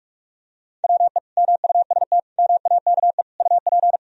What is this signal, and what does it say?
Morse code, heard as a keyed tone, about 41 wpm, 700 Hz. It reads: WE MUST MAKE UP
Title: Sentences from Top 1000 Words - Review Part 3 - 45wpm